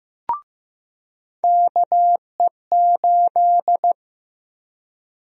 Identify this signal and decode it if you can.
Morse code, heard as a keyed tone, about 15 wpm, 700 Hz. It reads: KE8
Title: USA Call Sign Prefixes - Repeated 2x - 15wpm